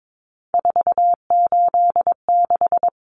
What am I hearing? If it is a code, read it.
Morse code, heard as a keyed tone, about 22 wpm, 700 Hz. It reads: 486
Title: Common Cut Numbers - 3-Digit 22wpm